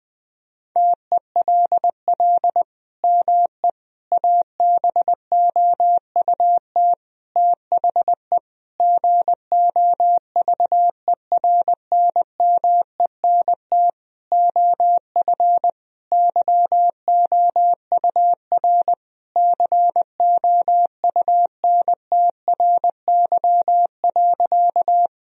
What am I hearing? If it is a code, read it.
Morse code, heard as a keyed tone, about 20 wpm, 700 Hz. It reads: TELL ME ABOUT THE GOVERNMENT OF YOUR COUNTRY.
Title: Sentences from Top 1000 Words - Review Part 3 - 20wpm